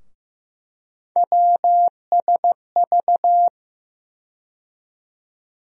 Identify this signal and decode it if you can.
Morse code, heard as a keyed tone, about 15 wpm, 700 Hz. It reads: WSV
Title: USA Call Sign Postfixes - 3 Letters 15wpm